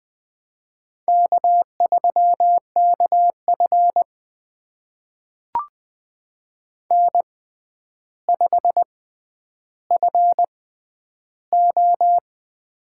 Morse code, heard as a keyed tone, about 20 wpm, spaced as for 10 wpm, 700 Hz. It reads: K3KF N5FO
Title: USA Call Signs - 6x Character Spacing - 20wpm